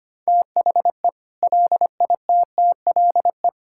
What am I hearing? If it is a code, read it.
Morse code, heard as a keyed tone, about 25 wpm, 700 Hz. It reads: THE LITTLE
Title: Sentences from Top 1600 Words 25wpm